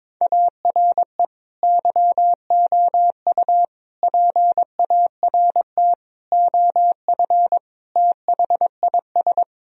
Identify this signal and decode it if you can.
Morse code, heard as a keyed tone, about 22 wpm, 700 Hz. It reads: ARE YOU PART OF THIS
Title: Sentences from Top 1000 Words - Review Part 2 - 22wpm